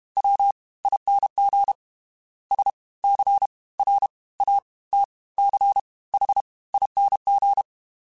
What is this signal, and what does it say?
Morse code, heard as a keyed tone, about 32 wpm, spaced as for 21 wpm, 800 Hz. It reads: WING SCRATCHING
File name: Special INGs / Special INGs-mp3s / Fast ing @30